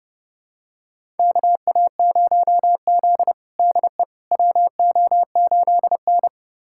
Morse code, heard as a keyed tone, about 30 wpm, 700 Hz. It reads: KA0Z DE WO8N
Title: USA Callsign DE Callsign - 4 Characters - 30wpm